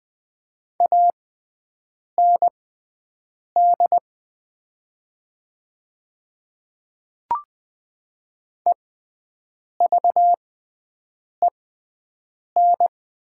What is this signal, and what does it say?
Morse code, heard as a keyed tone, about 20 wpm, spaced as for 7 wpm, 700 Hz. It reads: AND EVEN
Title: 6x Character Spacing - Top 200 Words - 20wpm